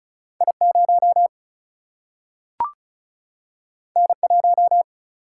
Morse code, heard as a keyed tone, about 36 wpm, 700 Hz. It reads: I 0 N 1
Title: Single-Letter Single-Number 35wpm